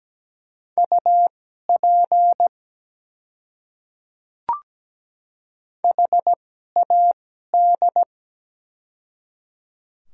Morse code, heard as a keyed tone, about 17 wpm, spaced as for 12 wpm, 700 Hz. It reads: UP HAD
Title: 2x Character Spacing - Top 100 Words - 17wpm